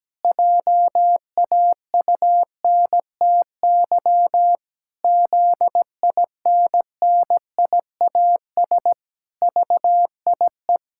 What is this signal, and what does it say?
Morse code, heard as a keyed tone, about 17 wpm, 700 Hz. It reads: JAUNTY ZINNIAS VIE
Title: Pangram Challenge v2 - 17wpm